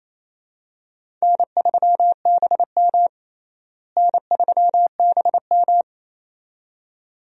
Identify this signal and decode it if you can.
Morse code, heard as a keyed tone, about 28 wpm, 700 Hz. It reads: N3BM N3BM
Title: USA Call Signs - 4 Characters - Repeated 2x - 28wpm